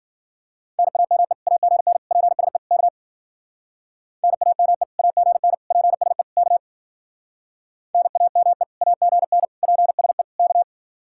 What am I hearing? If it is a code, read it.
Morse code, heard as a keyed tone, about 49 wpm, 700 Hz. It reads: NAME AGN PSE K NAME AGN PSE K NAME AGN PSE K
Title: Speed-Racing Common QSO Phrases - 40wpm